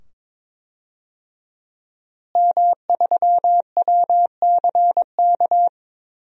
Morse code, heard as a keyed tone, about 22 wpm, 700 Hz. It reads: M3WCK
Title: DXCC-100 Call Signs - 5 Characters - Repeated 2x - 22wpm